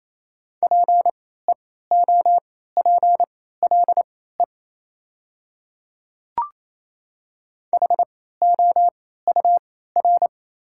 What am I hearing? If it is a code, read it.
Morse code, heard as a keyed tone, about 28 wpm, spaced as for 16 wpm, 700 Hz. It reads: PEOPLE HOUR
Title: 3x Character Spacing - Top 500 Words - 28wpm